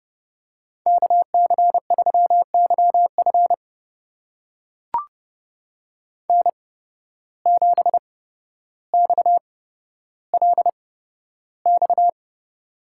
Morse code, heard as a keyed tone, about 31 wpm, spaced as for 13 wpm, 700 Hz. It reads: KC3YF N7XLX
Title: USA Call Signs - 5 Characters - 8x Character Spacing - 30wpm